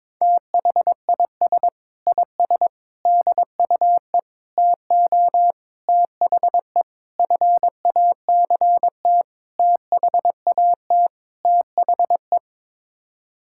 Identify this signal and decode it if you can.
Morse code, heard as a keyed tone, about 21 wpm, 700 Hz. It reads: THIS IS DUE TO THE FACT THAT THE
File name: practice / 22wpm / Sets of 8 Words - 22wpm